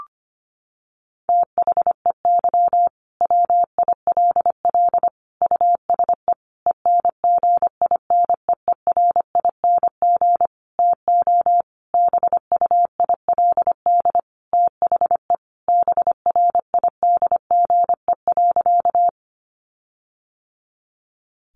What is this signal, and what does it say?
Morse code, heard as a keyed tone, about 25 wpm, 700 Hz. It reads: THEY WILL USE ENGINEERING TO BUILD THE BRIDGE.
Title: Sentences from Top 1500 Words 25wpm